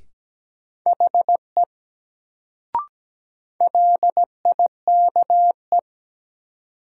Morse code, heard as a keyed tone, about 17 wpm, 700 Hz. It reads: HE LIKE
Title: Sentences from Top 100 Words - New Word Drill - 17wpm